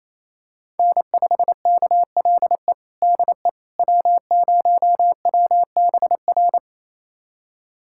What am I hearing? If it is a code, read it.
Morse code, heard as a keyed tone, about 28 wpm, 700 Hz. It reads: N5KLE DE W0WBR
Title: USA Callsign DE Callsign - 5 Characters - 28wpm